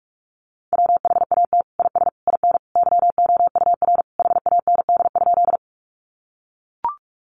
Morse code, heard as a keyed tone, about 44 wpm, 700 Hz. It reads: WHAT IS IN YOUR HAND?
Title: Sentences from Top 200 Words 45wpm